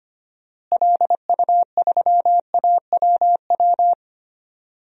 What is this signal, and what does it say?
Morse code, heard as a keyed tone, about 25 wpm, 700 Hz. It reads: LU3AWW
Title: DXCC-100 Call Signs - 6 Characters - Repeated 2x - 25wpm